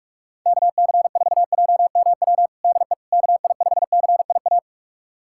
Morse code, heard as a keyed tone, about 46 wpm, 700 Hz. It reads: KC4JMW DE KI5CIA
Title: USA Callsign DE Callsign 45wpm